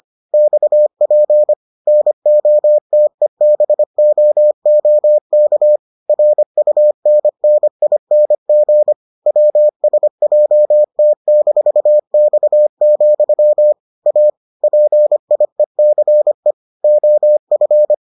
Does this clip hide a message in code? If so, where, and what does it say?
Yes: Morse code from 0.3 to 17.9 s, heard as a keyed tone, about 25 wpm, 600 Hz. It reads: XP NOTEBOOK RUNNING WSJT-X, A PIECE OF